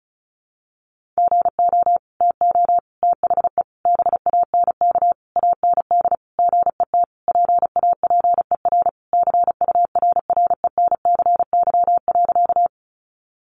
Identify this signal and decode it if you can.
Morse code, heard as a keyed tone, about 34 wpm, 700 Hz. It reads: GO TO THE BANK AND GET PAPER CURRENCY.
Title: Sentences from Top 1500 Words 35wpm